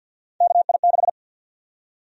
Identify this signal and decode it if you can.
Morse code, heard as a keyed tone, about 51 wpm, 700 Hz. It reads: KI6
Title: USA Call Sign Prefixes - Repeated 2x - 50wpm